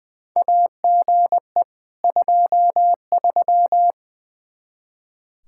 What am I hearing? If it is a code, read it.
Morse code, heard as a keyed tone, about 20 wpm, 700 Hz. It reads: AGE 23
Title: Practice QSO - Exchange 3 - More Info - 20wpm